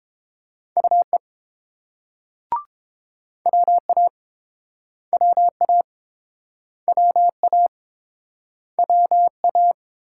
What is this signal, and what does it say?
Morse code, heard as a keyed tone, about 31 wpm, 700 Hz. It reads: UE WA WA WA WA